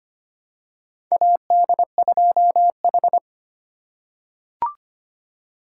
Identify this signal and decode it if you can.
Morse code, heard as a keyed tone, about 25 wpm, 700 Hz. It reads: AD2H
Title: USA Call Signs - 4 Characters - 4x Character Spacing - 25wpm